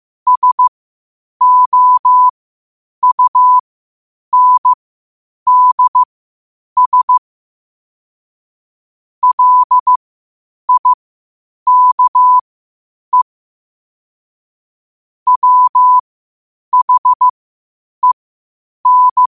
Morse code, heard as a keyed tone, about 15 wpm, spaced as for 8 wpm, 1000 Hz. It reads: SOUNDS LIKE WHEN
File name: qotdfiles / 8wpm